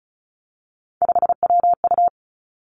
Morse code, heard as a keyed tone, about 36 wpm, 700 Hz. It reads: 5WU
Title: Three Challenging Letter-Number Combinations 35wpm